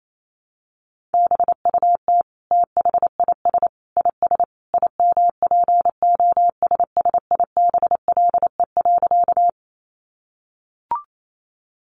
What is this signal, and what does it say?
Morse code, heard as a keyed tone, about 28 wpm, 700 Hz. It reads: BUT THIS IS IMPOSSIBLE.
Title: Sentences from Top 1100 Words 28wpm